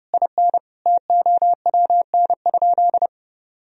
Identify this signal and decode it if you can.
Morse code, heard as a keyed tone, about 30 wpm, 700 Hz. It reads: IN TOWN?
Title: Sentences from Top 1000 Words - Review Part 4 - 30wpm